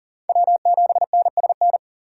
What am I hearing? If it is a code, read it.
Morse code, heard as a keyed tone, about 40 wpm, 700 Hz. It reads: W7NSN
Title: USA Call Signs - 5 Characters - 4x Character Spacing - 40wpm